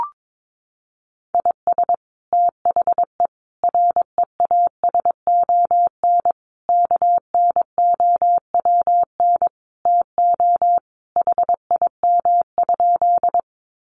Morse code, heard as a keyed tone, about 22 wpm, 700 Hz. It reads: IS THE REASON KNOWN TO HIM?
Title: Sentences from Top 700 Words 22wpm